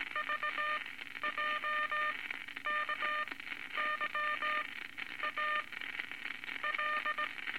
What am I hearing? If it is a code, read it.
Morse code, heard as a keyed tone, about 18 wpm, spaced as for 11 wpm, 1210 Hz, with noise. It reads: VJKYA L
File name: Morse Code Records / 19 A LCM 166 Ten Words Per Minute